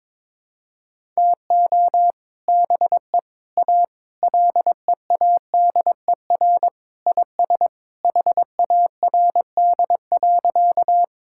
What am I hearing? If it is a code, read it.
Morse code, heard as a keyed tone, about 22 wpm, 700 Hz. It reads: TO BE A LEADER IS HARD.
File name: practice / 22wpm / Sentences with 3-Word Phrases - 22wpm